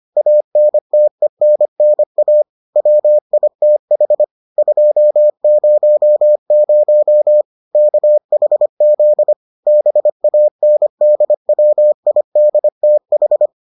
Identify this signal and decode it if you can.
Morse code, heard as a keyed tone, about 25 wpm, 600 Hz. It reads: ANTENNA WITH 200 KHZ BANDWIDTH